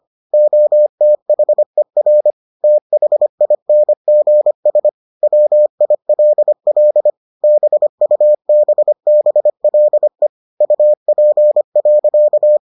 Morse code, heard as a keyed tone, about 25 wpm, 600 Hz. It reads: OTHER THINGS WILL BUBBLE UP.